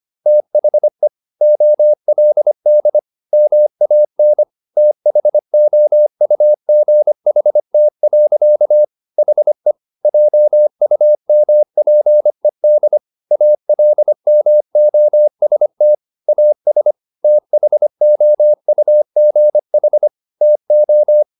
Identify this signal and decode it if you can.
Morse code, heard as a keyed tone, about 25 wpm, 600 Hz. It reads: THE OLD MAN THOUGHT. HE JUMPED ALMOST AS THOUGH TO
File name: the old man and the sea part 1 25wpm